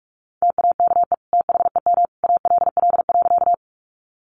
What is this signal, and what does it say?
Morse code, heard as a keyed tone, about 44 wpm, 700 Hz. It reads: TAKE THEM ALL.
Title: Sentences from Top 200 Words 45wpm